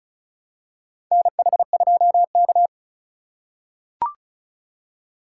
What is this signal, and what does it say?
Morse code, heard as a keyed tone, about 36 wpm, 700 Hz. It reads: NH2K
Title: USA Call Signs - 4 Characters - 3x Character Spacing - 35wpm